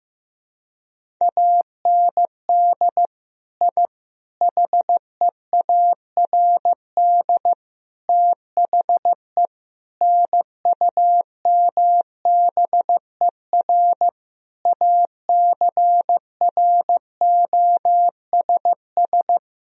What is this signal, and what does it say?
Morse code, heard as a keyed tone, about 15 wpm, 700 Hz. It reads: AND I HEARD THE NUMBER ACROSS
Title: Sentences from Top 1000 Words - Review Part 1 - 15wpm